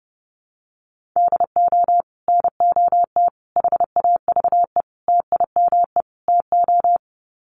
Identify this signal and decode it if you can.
Morse code, heard as a keyed tone, about 30 wpm, 700 Hz. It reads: DO NOT HAVE TIME TO
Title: Sets of 5 Words 30wpm